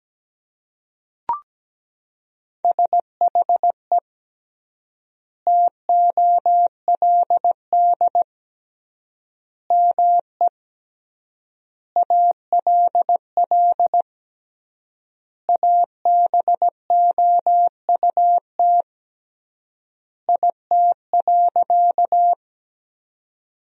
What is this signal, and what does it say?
Morse code, heard as a keyed tone, about 17 wpm, 700 Hz. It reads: SHE TOLD ME ALL ABOUT IT.
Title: 3x Word-Spacing - Sentences from Top 400 Words - 17wpm